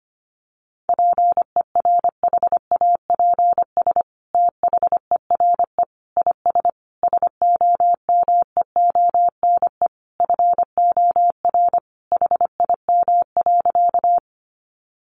Morse code, heard as a keyed tone, about 25 wpm, 700 Hz. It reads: PERHAPS THERE IS SOMEONE FOR HIM.
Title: Sentences from Top 600 Words 25wpm